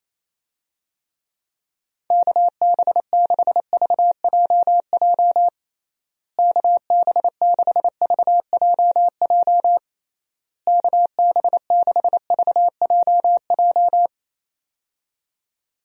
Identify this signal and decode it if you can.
Morse code, heard as a keyed tone, about 27 wpm, 700 Hz. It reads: KB6VJJ KB6VJJ KB6VJJ
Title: USA Call Signs - 6 Characters - Repeated 3x - 28wpm